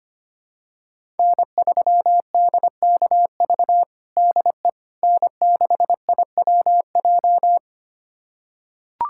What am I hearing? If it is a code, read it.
Morse code, heard as a keyed tone, about 25 wpm, 700 Hz. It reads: N3DKV DE N6IWJ
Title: USA Callsign DE Callsign - 5 Characters - 25wpm